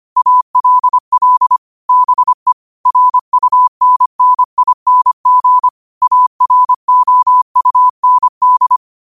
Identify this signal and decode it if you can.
Morse code, heard as a keyed tone, about 25 wpm, 1000 Hz. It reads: ALL BE RUNNING AROUND
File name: qotdfiles / 25wpm